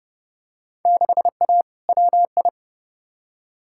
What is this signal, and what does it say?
Morse code, heard as a keyed tone, about 30 wpm, 700 Hz. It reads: 6A WI
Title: Field Day Exchanges - 30wpm